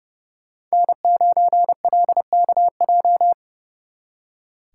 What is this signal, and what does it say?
Morse code, heard as a keyed tone, about 35 wpm, 700 Hz. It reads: N9LKJ
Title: Speed Racing - USA 1x3 Call Signs 30wpm